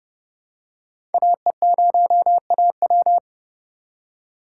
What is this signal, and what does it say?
Morse code, heard as a keyed tone, about 30 wpm, 700 Hz. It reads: AE0AW
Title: USA Call Signs - 5 Characters - 3x Character Spacing - 30wpm